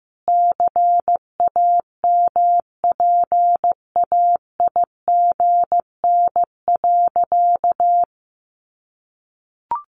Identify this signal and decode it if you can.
Morse code, heard as a keyed tone, about 15 wpm, 700 Hz. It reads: CAMPAIGN.
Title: Sentences from Top 1200 Words 15wpm